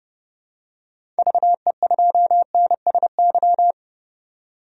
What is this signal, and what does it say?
Morse code, heard as a keyed tone, about 30 wpm, 700 Hz. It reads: VE2NSY
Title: DXCC-100 Call Signs - 6 Characters - 30wpm